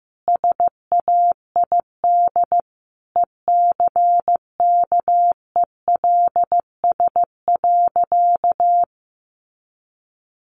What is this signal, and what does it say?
Morse code, heard as a keyed tone, about 15 wpm, 700 Hz. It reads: SAID ECKELS.
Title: A Sound Of Thunder 15wpm